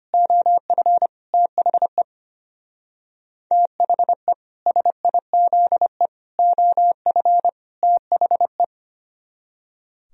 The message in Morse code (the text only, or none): OF THE THE SIZE OF THE